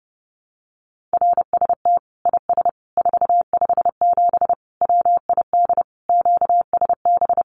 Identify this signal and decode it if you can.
Morse code, heard as a keyed tone, about 30 wpm, 700 Hz. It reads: RST IS 457 WID QSB
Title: Common QSO Phrases 30wpm